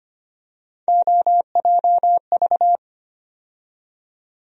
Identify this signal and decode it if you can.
Morse code, heard as a keyed tone, about 25 wpm, 700 Hz. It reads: OJV
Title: USA Call Sign Postfixes - 3 Letters 25wpm